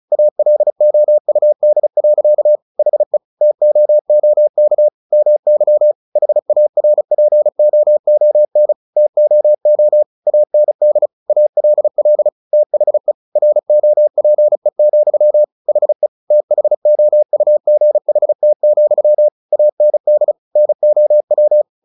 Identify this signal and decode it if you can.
Morse code, heard as a keyed tone, about 34 wpm, 600 Hz. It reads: ALOUD. HE TOOK MY HARPOON TOO AND ALL THE ROPE, HE THOUGHT, AND NOW
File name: the old man and the sea 2 35wpm